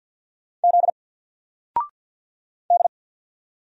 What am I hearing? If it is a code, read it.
Morse code, heard as a keyed tone, about 48 wpm, 700 Hz. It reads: Z D
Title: Single Character (Letter:Number:Punctuation:Prosign) 50wpm